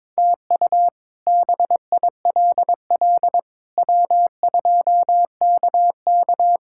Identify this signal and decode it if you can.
Morse code, heard as a keyed tone, about 22 wpm, 700 Hz. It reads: TU BILL W2KK
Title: K1USN SST QSO Practice - 22wpm